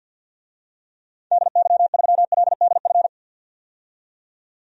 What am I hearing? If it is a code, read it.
Morse code, heard as a keyed tone, about 50 wpm, 700 Hz. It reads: DY3LDF
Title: DXCC-100 Call Signs - 6 Characters - 50wpm